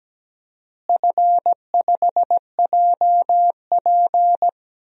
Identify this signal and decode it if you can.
Morse code, heard as a keyed tone, about 17 wpm, 700 Hz. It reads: F5JP
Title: DXCC-100 Call Signs - 4 Characters - Repeated 2x - 17wpm